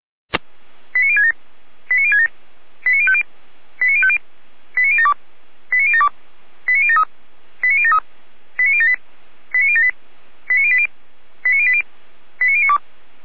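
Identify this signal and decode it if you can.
Five-tone selective call (ZVEI), heard as quick runs of five tones, tones of about 80 ms. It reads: 89067 89067 89050 89050 89072 89072 89073 89073 89078 89078 89080 89080 89092
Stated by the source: ZVEI five-tone selective call 89031, received via SDR.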